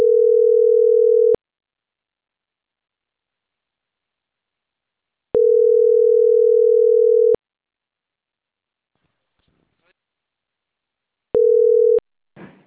Ringback tone (North America), 440 + 480 Hz.